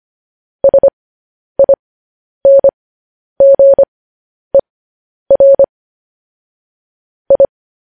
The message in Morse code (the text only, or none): SINGER I